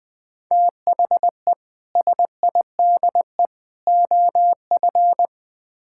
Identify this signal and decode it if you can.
Morse code, heard as a keyed tone, about 20 wpm, 700 Hz. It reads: THE SIDE OF
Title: Build A Phrase - 5-Word Phrases - 20wpm